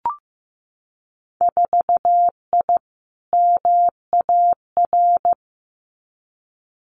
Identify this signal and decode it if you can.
Morse code, heard as a keyed tone, about 15 wpm, 700 Hz. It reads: 4I MAR
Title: Winter Field Day Exchanges Encore 15wpm